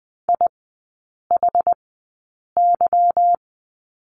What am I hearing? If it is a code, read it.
Morse code, heard as a keyed tone, about 20 wpm, spaced as for 8 wpm, 700 Hz. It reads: IHY
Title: 145 Lesson 24e - Letter y - Three Characters 20wpm